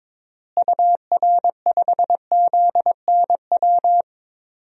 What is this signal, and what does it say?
Morse code, heard as a keyed tone, about 22 wpm, 700 Hz. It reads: UR5ZNW
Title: DXCC-100 Call Signs - 6 Characters - Repeated 3x - 22wpm